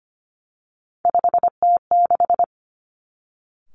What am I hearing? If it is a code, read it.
Morse code, heard as a keyed tone, about 25 wpm, 700 Hz. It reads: 5T6
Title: Three Challenging Letter-Number Combinations 25wpm